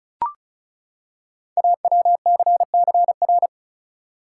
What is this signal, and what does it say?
Morse code, heard as a keyed tone, about 36 wpm, 690 Hz. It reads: AWCCR
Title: Code Groups - 5 Letters - 35wpm